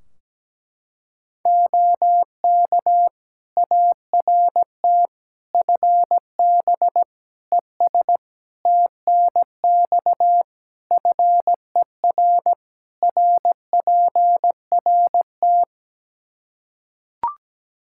Morse code, heard as a keyed tone, about 17 wpm, 700 Hz. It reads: OK ART FB ES TNX FER RPRT
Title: Practice QSO - Exchange 2 - Info - 17wpm